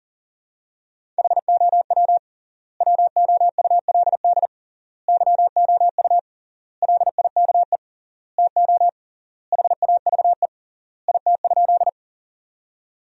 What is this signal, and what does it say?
Morse code, heard as a keyed tone, about 40 wpm, 700 Hz. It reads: HOW WOULD YOU LIKE TO HAVE IT?
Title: 3x Word-Spacing - Sentences from Top 100 Words - 40wpm